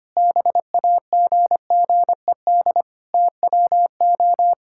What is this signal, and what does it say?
Morse code, heard as a keyed tone, about 25 wpm, 700 Hz. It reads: BAGGED TWO
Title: Pangram Challenge v2 - 25wpm